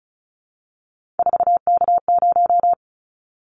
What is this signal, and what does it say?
Morse code, heard as a keyed tone, about 36 wpm, 700 Hz. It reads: 4K0
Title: Callsign Prefixes for Top 100 DXCC - 35wpm